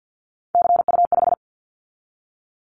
Morse code, heard as a keyed tone, about 50 wpm, 700 Hz. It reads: CU5
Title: Callsign Prefixes for Top 100 DXCC - 50wpm